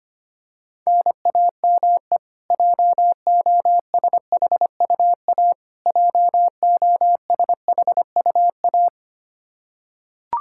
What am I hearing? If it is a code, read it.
Morse code, heard as a keyed tone, about 25 wpm, 700 Hz. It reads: NAME JOSHUA JOSHUA
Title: Practice QSO - Exchange 1 - Beginning - 25wpm